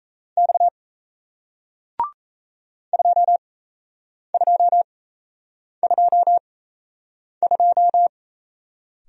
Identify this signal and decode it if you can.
Morse code, heard as a keyed tone, about 38 wpm, spaced as for 8 wpm, 700 Hz. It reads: X 2222